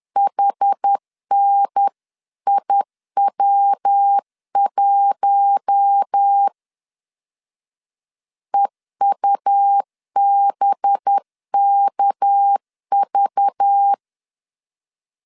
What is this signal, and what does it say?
Morse code, heard as a keyed tone, about 10 wpm, 800 Hz. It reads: HNIW1 EUBKV
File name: CW / DET / 1 6 1500C S B 10wpm 4137